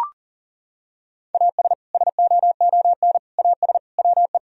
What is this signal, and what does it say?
Morse code, heard as a keyed tone, about 40 wpm, 700 Hz. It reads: AS SOON AS WE
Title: Sentences from Top 1000 Words - Review Part 2 - 40wpm